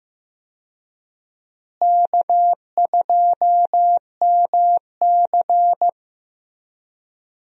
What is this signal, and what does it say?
Morse code, heard as a keyed tone, about 15 wpm, 700 Hz. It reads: K2MC
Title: USA Call Signs - 4 Characters - Repeated 2x - 15wpm